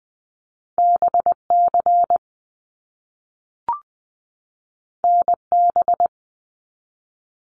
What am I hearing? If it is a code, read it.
Morse code, heard as a keyed tone, about 20 wpm, 700 Hz. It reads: BC NB
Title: Canadian Provinces and Territories 20wpm